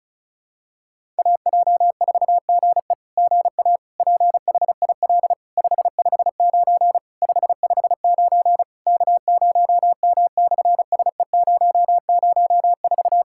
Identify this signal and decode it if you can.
Morse code, heard as a keyed tone, about 33 wpm, 700 Hz. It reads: AJ4GE GA PHIL 559 559 K0M/SE004
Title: SOTA QSO Practice 35wpm